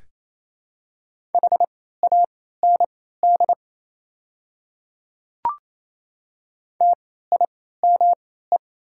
Morse code, heard as a keyed tone, about 28 wpm, spaced as for 16 wpm, 700 Hz. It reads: HAND TIME
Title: 3x Character Spacing - Top 500 Words - 28wpm